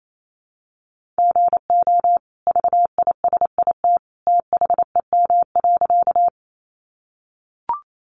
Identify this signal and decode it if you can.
Morse code, heard as a keyed tone, about 28 wpm, 700 Hz. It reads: GO VISIT THEM.
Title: Sentences from Top 900 Words 28wpm